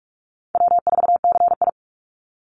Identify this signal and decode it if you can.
Morse code, heard as a keyed tone, about 44 wpm, 700 Hz. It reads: W4CI
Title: Call Signs Repeated 3x - 45wpm